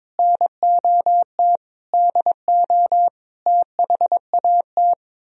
Morse code, heard as a keyed tone, about 23 wpm, 700 Hz. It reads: NOT DO THAT
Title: Speed-Racing - Sets of 5 Words 22wpm